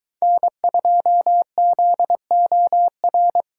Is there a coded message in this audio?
Yes: Morse code from 0.2 to 3.4 s, heard as a keyed tone, about 22 wpm, 700 Hz. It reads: N2ZOR